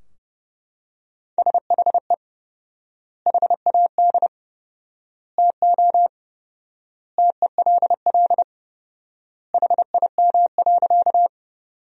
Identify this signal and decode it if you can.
Morse code, heard as a keyed tone, about 30 wpm, 700 Hz. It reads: SHE HAD TO TELL HIM.
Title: 4x Word-Spacing - Sentences from Top 200 Words - 30wpm